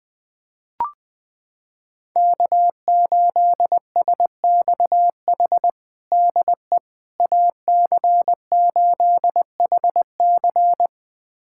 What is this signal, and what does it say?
Morse code, heard as a keyed tone, about 21 wpm, 700 Hz. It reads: K8SXH DE AC8HC
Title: USA Callsign DE Callsign 20wpm